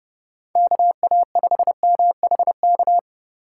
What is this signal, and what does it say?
Morse code, heard as a keyed tone, about 32 wpm, 700 Hz. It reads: KA5MHK
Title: USA Call Signs - 6 Characters - 4x Character Spacing - 30wpm